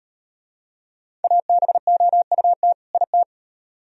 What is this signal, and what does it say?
Morse code, heard as a keyed tone, about 37 wpm, 700 Hz. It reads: ABOUT IT